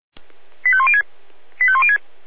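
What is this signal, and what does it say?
Five-tone selective call (ZVEI), heard as quick runs of five tones, tones of about 70 ms. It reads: 85106 85106
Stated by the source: ZVEI five-tone selective call received on SDR.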